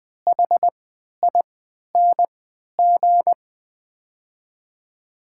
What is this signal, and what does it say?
Morse code, heard as a keyed tone, about 20 wpm, spaced as for 11 wpm, 700 Hz. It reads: HING
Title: 3x Character Spacing - Top 500 Words - 20wpm